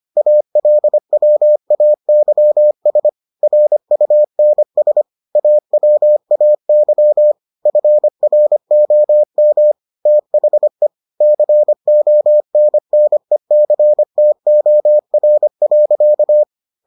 Morse code, heard as a keyed tone, about 25 wpm, 600 Hz. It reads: ALWAYS RUNS AWAY FROM THE CONNECTOR.